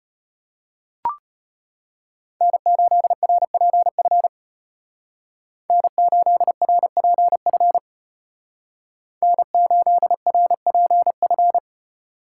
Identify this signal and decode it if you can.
Morse code, heard as a keyed tone, about 31 wpm, 700 Hz. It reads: N8RPF N8RPF N8RPF
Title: Speed Racing - USA 1x3 Call Signs 25wpm